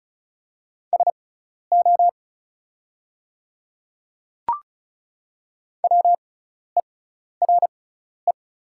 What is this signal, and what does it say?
Morse code, heard as a keyed tone, about 36 wpm, spaced as for 12 wpm, 700 Hz. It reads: SO WERE
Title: 6x Character Spacing - Top 100 Words - 35wpm